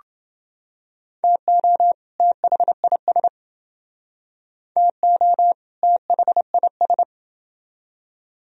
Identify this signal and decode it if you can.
Morse code, heard as a keyed tone, about 29 wpm, 700 Hz. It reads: TO THIS TO THIS